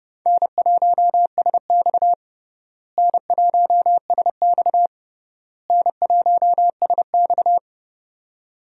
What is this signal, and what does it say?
Morse code, heard as a keyed tone, about 30 wpm, 700 Hz. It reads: N1SX N1SX N1SX
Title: USA Call Signs - 4 Characters - Repeated 3x - 30wpm